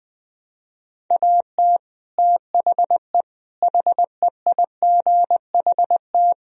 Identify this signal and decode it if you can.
Morse code, heard as a keyed tone, about 21 wpm, 700 Hz. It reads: AT THE HEIGHT
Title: Speed-Racing - Sets of 5 Words 20wpm